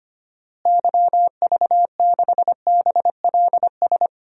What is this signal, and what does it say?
Morse code, heard as a keyed tone, about 25 wpm, 700 Hz. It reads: YV6BLS
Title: DXCC-100 Call Signs - 6 Characters - Repeated 2x - 25wpm